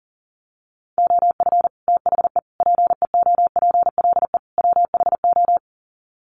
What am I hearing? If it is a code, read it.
Morse code, heard as a keyed tone, about 40 wpm, 700 Hz. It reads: OF THE PEOPLE WHO
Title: Sets of 4 Words 40wpm